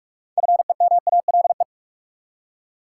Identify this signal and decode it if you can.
Morse code, heard as a keyed tone, about 46 wpm, 700 Hz. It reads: FEMALE
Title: Six Letter Words - 45wpm